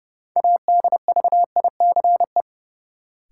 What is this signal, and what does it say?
Morse code, heard as a keyed tone, about 30 wpm, 700 Hz. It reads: ADVICE
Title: Sentences from Top 1100 Words - New Word Drill Alt - 30wpm